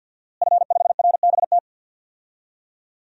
Morse code, heard as a keyed tone, about 52 wpm, 700 Hz. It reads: F5RBT
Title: DXCC-100 Call Signs - 5 Characters - Repeated 3x - 50wpm